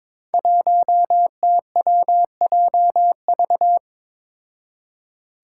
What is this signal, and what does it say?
Morse code, heard as a keyed tone, about 21 wpm, 700 Hz. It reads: 1TWJV
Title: Five Challenging Letter-Number Combinations 22wpm